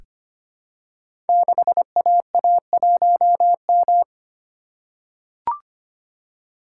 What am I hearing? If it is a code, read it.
Morse code, heard as a keyed tone, about 25 wpm, 700 Hz. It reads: 6AA1M
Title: Code Groups - 5 Letter:Numbers - 25wpm